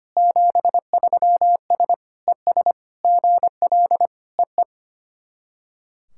Morse code, heard as a keyed tone, about 25 wpm, 700 Hz. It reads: 73S ES GL EE
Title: SOTA QSO Practice 25wpm